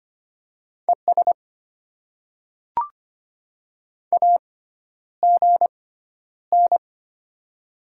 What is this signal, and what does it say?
Morse code, heard as a keyed tone, about 25 wpm, spaced as for 11 wpm, 700 Hz. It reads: ES AGN
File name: Top 20 QSO Elements - 6x Character Spacing - 25wpm